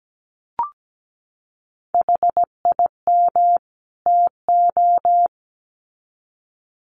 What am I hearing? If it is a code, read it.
Morse code, heard as a keyed tone, about 17 wpm, 700 Hz. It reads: HIM TO
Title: Sets of 2 Words 17wpm